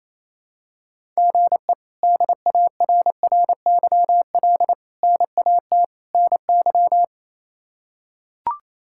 Morse code, heard as a keyed tone, about 27 wpm, 700 Hz. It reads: GE DARRYL NAT NY
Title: K1USN SST QSO Practice - 28wpm